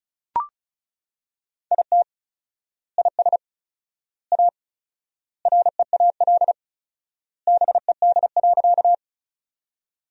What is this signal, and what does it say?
Morse code, heard as a keyed tone, about 34 wpm, 700 Hz. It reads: IT IS A REAL BED.
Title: 4x Word-Spacing - Sentences from Top 500 Words - 35wpm